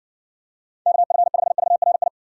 Morse code, heard as a keyed tone, about 60 wpm, 700 Hz. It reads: KF5VRI